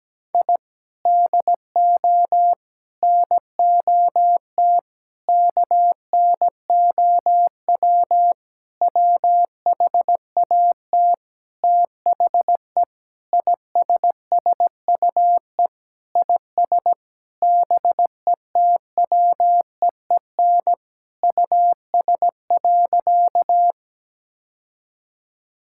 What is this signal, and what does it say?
Morse code, heard as a keyed tone, about 17 wpm, 700 Hz. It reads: I DO NOT KNOW WHAT THE ISSUE IS BETWEEN US.
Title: Sentences from Top 1000 Words - Review Part 6 - 17wpm